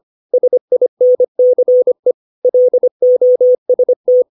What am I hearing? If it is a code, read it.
Morse code, heard as a keyed tone, about 25 wpm, 500 Hz. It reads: SINCE LOST
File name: 632 words / 632 Words 25WPM